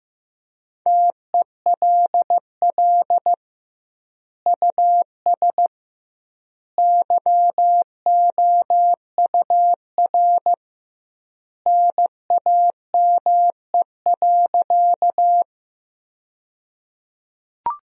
Morse code, heard as a keyed tone, about 15 wpm, 700 Hz. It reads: TELL US YOUR NAME.
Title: 2x Word-Spacing - Sentences from Top 200 Words - 15wpm